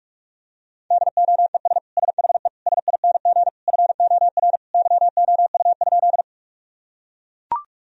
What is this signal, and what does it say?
Morse code, heard as a keyed tone, about 41 wpm, 700 Hz. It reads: DOES SHE SING FOR YOU?
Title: Sentences from Top 1000 Words - Review Part 2 - 45wpm